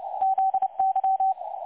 Morse code, heard as a keyed tone, about 29 wpm, 770 Hz. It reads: ZY
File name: dlazaawansowanych / nagraniazpasma / websdr recording 3565.0kHz